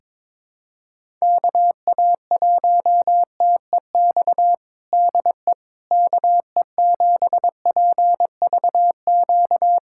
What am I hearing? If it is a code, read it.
Morse code, heard as a keyed tone, about 22 wpm, 700 Hz. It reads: KA1TEX DE KE7PVQ
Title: USA Callsign DE Callsign 22wpm